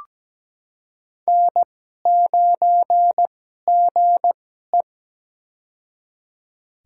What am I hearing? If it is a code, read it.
Morse code, heard as a keyed tone, about 17 wpm, spaced as for 12 wpm, 700 Hz. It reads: N9GE
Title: USA Call Signs - 4 Characters - 2x Character Spacing - 17wpm